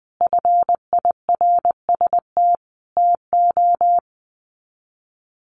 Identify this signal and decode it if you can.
Morse code, heard as a keyed tone, about 20 wpm, 700 Hz. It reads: FIRST TO
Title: Sets of 5 Words Encore 20wpm